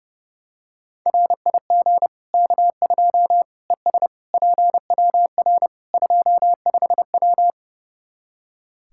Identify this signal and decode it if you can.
Morse code, heard as a keyed tone, about 32 wpm, 700 Hz. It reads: RIG K2 ES PWR 25W
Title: Practice QSO - Exchange 2 - Info - 30wpm